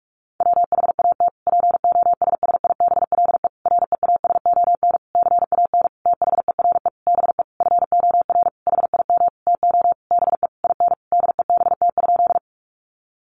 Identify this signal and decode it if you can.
Morse code, heard as a keyed tone, about 44 wpm, 700 Hz. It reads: WHAT POSSIBLE REASON CAN THERE BE FOR HIM TO BE IN DEBT?
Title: Sentences from Top 1300 Words 45wpm